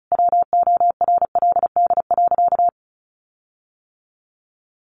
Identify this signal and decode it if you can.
Morse code, heard as a keyed tone, about 34 wpm, 700 Hz. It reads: WORLD.
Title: Sentences from Top 1400 Words 35wpm